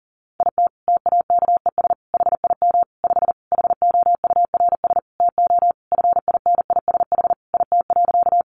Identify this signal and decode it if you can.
Morse code, heard as a keyed tone, about 40 wpm, 700 Hz. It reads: IT TAKES HIM 5 HOURS TO FINISH IT.
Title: Additional Easy Phrases 40wpm